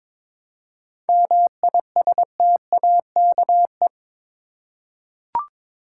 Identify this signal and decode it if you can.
Morse code, heard as a keyed tone, about 22 wpm, 700 Hz. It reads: MISTAKE
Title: Seven Letter Words - 22wpm